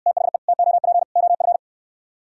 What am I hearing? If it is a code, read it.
Morse code, heard as a keyed tone, about 63 wpm, 700 Hz. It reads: THE TOP OF